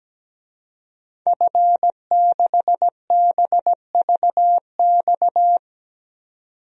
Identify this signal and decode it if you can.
Morse code, heard as a keyed tone, about 17 wpm, 700 Hz. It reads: F6BVX
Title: DXCC-100 Call Signs - 5 Characters - 17wpm